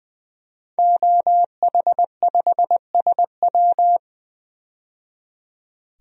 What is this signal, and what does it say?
Morse code, heard as a keyed tone, about 20 wpm, 700 Hz. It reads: OH5SW
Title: DXCC-100 Call Signs - 5 Characters - 20wpm